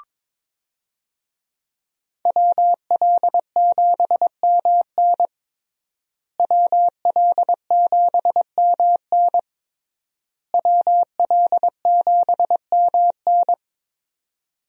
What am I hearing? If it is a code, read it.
Morse code, heard as a keyed tone, about 22 wpm, 700 Hz. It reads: WL7MN WL7MN WL7MN
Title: USA Call Signs - 5 Characters - Repeated 3x - 22wpm